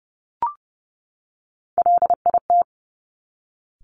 Morse code, heard as a keyed tone, about 30 wpm, 700 Hz. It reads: LIT
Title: Top Three Letter Combinations in Words 30wpm